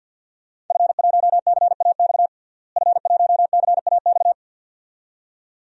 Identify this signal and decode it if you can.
Morse code, heard as a keyed tone, about 46 wpm, 700 Hz. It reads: F1CAX F1CAX
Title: DXCC-100 Call Signs - 5 Characters - Repeated 2x - 50wpm